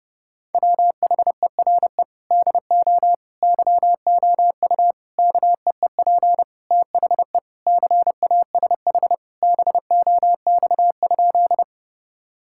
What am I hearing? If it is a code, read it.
Morse code, heard as a keyed tone, about 32 wpm, 700 Hz. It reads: WHERE DO YOU KEEP THE CASH BOX?
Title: Sentences from Top 1000 Words - Review Part 5 - 30wpm